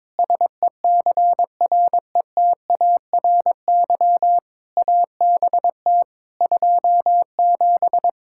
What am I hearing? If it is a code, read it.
Morse code, heard as a keyed tone, about 22 wpm, 700 Hz. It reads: SECRETARY ABT 27
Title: Practice QSO - Exchange 3 - More Info - 22wpm